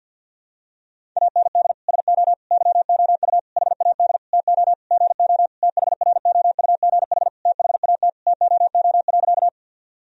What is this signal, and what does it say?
Morse code, heard as a keyed tone, about 52 wpm, 700 Hz. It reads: AND SO YOU HAD TO GO THROUGH THAT TOO.
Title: Sentences from Top 1000 Words - Review Part 1 - 50wpm